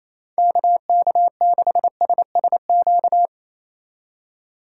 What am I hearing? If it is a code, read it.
Morse code, heard as a keyed tone, about 27 wpm, 700 Hz. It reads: KK6SSQ
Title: USA Call Signs - 6 Characters - 3x Character Spacing - 28wpm